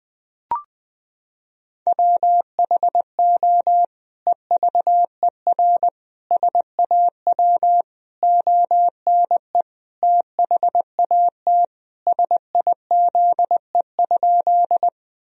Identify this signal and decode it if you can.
Morse code, heard as a keyed tone, about 20 wpm, 700 Hz. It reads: WHO EVER SAW ONE THAT SIZE?
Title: Sentences from Top 1000 Words - Review Part 2 - 20wpm